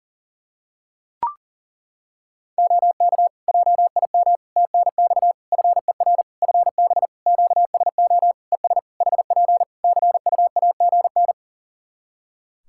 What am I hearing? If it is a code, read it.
Morse code, heard as a keyed tone, about 40 wpm, 700 Hz. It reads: OK JIM TNX FER FB QSO ES HP CUAGN
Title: Practice QSO - Exchange 4 - Ending - 40wpm